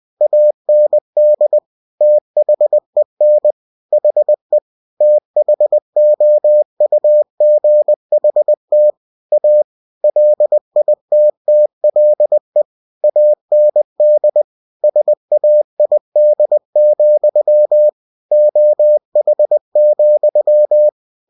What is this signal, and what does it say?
Morse code, heard as a keyed tone, about 20 wpm, 600 Hz. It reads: AND THEN HE THOUGHT A LITTLE AND SAID, OH,
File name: cw-books / winnie-the-pooh / winnie-20wpm-page-0040